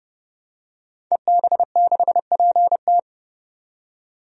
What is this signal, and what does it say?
Morse code, heard as a keyed tone, about 30 wpm, 700 Hz. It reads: EB6PT
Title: DXCC-100 Call Signs - 5 Characters - 30wpm